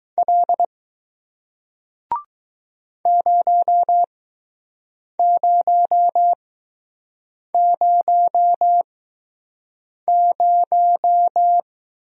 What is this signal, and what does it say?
Morse code, heard as a keyed tone, about 20 wpm, spaced as for 6 wpm, 700 Hz. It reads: L 0000